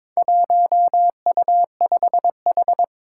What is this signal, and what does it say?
Morse code, heard as a keyed tone, about 22 wpm, 700 Hz. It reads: 1U5H
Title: Four Challenging Letter-Number Combinations 22wpm